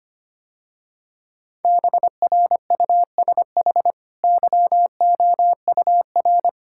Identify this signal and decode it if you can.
Morse code, heard as a keyed tone, about 25 wpm, 700 Hz. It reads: BRUSH YOUR
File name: Easy Sentences with Hints - 25wpm